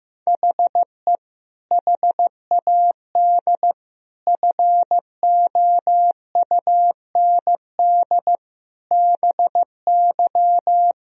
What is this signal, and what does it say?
Morse code, heard as a keyed tone, about 15 wpm, 700 Hz. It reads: HE HAD FOUND BY
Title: Sentences from Top 1000 Words - Review Part 3 - 15wpm